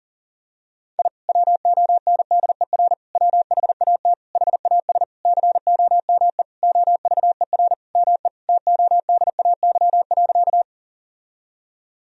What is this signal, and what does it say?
Morse code, heard as a keyed tone, about 37 wpm, 700 Hz. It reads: I WONDER WHAT HAS COME OVER ME TODAY.
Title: Sentences from Top 1000 Words - Review Part 4 - 40wpm